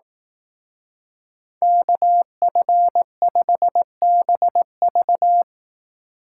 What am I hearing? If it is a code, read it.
Morse code, heard as a keyed tone, about 18 wpm, 700 Hz. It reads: KF5BV
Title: Speed Racing - USA 2x2 Call Signs 15wpm